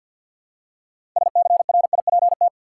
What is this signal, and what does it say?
Morse code, heard as a keyed tone, about 50 wpm, 700 Hz. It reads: SCRIPT
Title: Six Letter Words - 50wpm